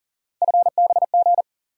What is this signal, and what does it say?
Morse code, heard as a keyed tone, about 40 wpm, 700 Hz. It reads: FBG
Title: USA Call Sign Postfixes - 3 Letters 40wpm